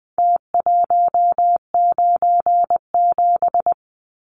Morse code, heard as a keyed tone, about 20 wpm, 700 Hz. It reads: T197
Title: Numbers - Contest Style - 4-Digit 20wpm